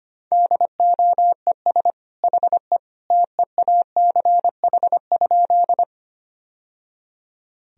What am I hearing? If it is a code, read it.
Morse code, heard as a keyed tone, about 25 wpm, 700 Hz. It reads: DOES HE TEACH?
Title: Sentences from Top 1000 Words - Review Part 2 - 25wpm